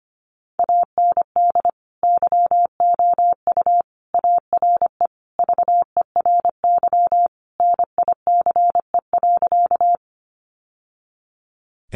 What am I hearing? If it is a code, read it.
Morse code, heard as a keyed tone, about 25 wpm, 700 Hz. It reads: AND YOU ARE VERY NICE.
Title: Sentences from Top 1100 Words 25wpm